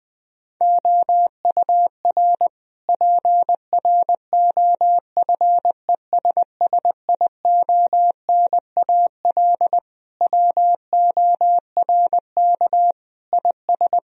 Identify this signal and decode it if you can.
Morse code, heard as a keyed tone, about 21 wpm, 700 Hz. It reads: OUR PROFESSIONAL WORK IS
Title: Sentences from Top 1000 Words - Review Part 4 - 20wpm